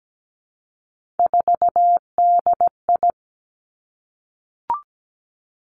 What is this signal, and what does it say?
Morse code, heard as a keyed tone, about 17 wpm, 700 Hz. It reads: 4DI
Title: Three Challenging Letter-Number Combinations 17wpm